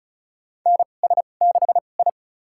Morse code, heard as a keyed tone, about 36 wpm, spaced as for 25 wpm, 700 Hz. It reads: NS6I
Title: USA Call Signs - 2x Character Spacing - 35wpm